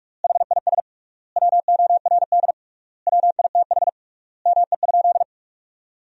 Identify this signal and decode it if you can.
Morse code, heard as a keyed tone, about 46 wpm, 700 Hz. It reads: HIS WORD WITH ME?
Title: 3x Word-Spacing - Sentences from Top 300 Words - 45wpm